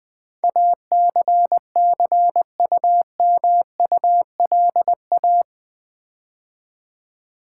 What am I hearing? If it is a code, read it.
Morse code, heard as a keyed tone, about 20 wpm, 700 Hz. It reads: ACCUMULA
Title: Build A Word - 4 Letters At A Time - 12-Letter Words - 20wpm